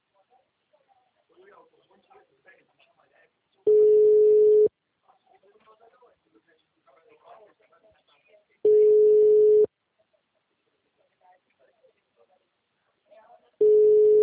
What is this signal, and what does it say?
Ringback tone (Europe, 425 Hz), with speech.